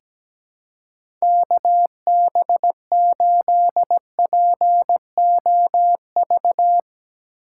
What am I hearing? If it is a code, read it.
Morse code, heard as a keyed tone, about 17 wpm, 700 Hz. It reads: KB8POV
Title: USA Call Signs - 6 Characters - 4x Character Spacing - 17wpm